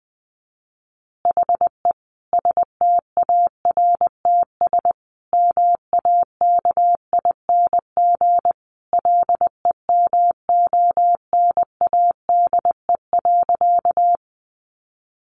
Morse code, heard as a keyed tone, about 20 wpm, 700 Hz. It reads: HE STARTS MAKING LEMONADE.
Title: Additional Easy Phrases 20wpm